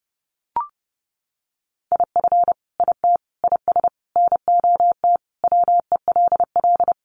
Morse code, heard as a keyed tone, about 30 wpm, 700 Hz. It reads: IF IT IS NOT WELL
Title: Sentences from Top 1200 Words 30wpm